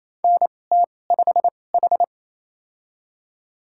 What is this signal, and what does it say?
Morse code, heard as a keyed tone, about 28 wpm, spaced as for 20 wpm, 700 Hz. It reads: NT5H
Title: USA Call Signs - 2x Character Spacing - 28wpm